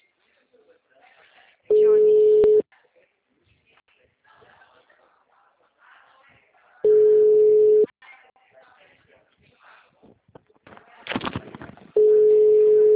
Ringback tone (Europe, 425 Hz).